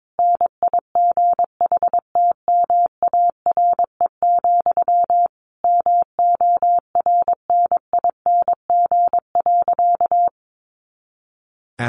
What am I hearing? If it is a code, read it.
Morse code, heard as a keyed tone, about 22 wpm, 700 Hz. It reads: NIGHTMARE, MORNING.
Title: A Sound Of Thunder 22wpm